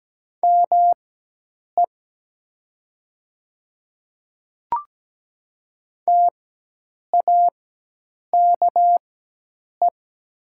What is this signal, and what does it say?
Morse code, heard as a keyed tone, about 17 wpm, spaced as for 8 wpm, 700 Hz. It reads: ME TAKE
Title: 4x Character Spacing - Top 500 Words - 17wpm